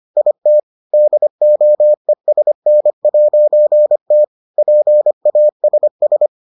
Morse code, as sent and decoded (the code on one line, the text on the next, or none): .. - / -.. --- . ... -. .----. - / .--. .- ... ...
IT DOESN'T PASS